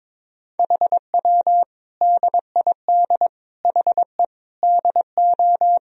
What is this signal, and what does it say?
Morse code, heard as a keyed tone, about 22 wpm, 700 Hz. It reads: HW DID HE DO
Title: Sentences from Top 100 Words and Abbreviations 22wpm